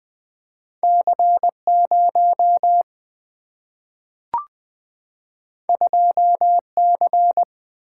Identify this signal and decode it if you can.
Morse code, heard as a keyed tone, about 21 wpm, 700 Hz. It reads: C 0 2 C
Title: Single-Letter Single-Number 20wpm